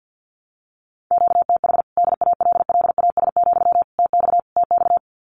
Morse code, heard as a keyed tone, about 50 wpm, 700 Hz. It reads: QTH DALLAS, TX TX
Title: Common QSO Phrases Encore 50wpm